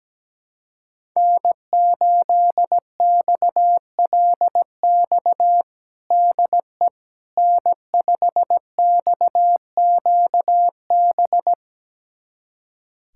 Morse code, heard as a keyed tone, about 17 wpm, 700 Hz. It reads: N8XLX DE N5XQB
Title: USA Callsign DE Callsign - 5 Characters - 17wpm